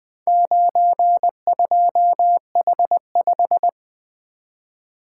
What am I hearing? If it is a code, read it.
Morse code, heard as a keyed tone, about 20 wpm, 700 Hz. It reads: 92H5
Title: Four Letter-Number Combinations - 20wpm